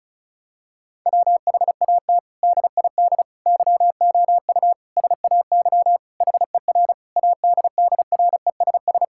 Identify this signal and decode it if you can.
Morse code, heard as a keyed tone, about 36 wpm, 700 Hz. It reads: WHAT DID YOU SAY HER ADDRESS
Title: Sentences from Top 1000 Words - Review Part 6 - 35wpm